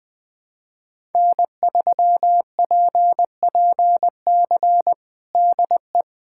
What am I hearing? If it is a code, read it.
Morse code, heard as a keyed tone, about 20 wpm, 700 Hz. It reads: N3PPC DE
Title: Practice QSO - Exchange 4 - Ending - 20wpm